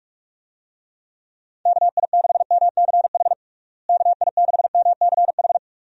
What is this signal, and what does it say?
Morse code, heard as a keyed tone, about 46 wpm, 700 Hz. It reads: KI6MCH KI6MCH
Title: USA Call Signs - 6 Characters - Repeated 2x - 45wpm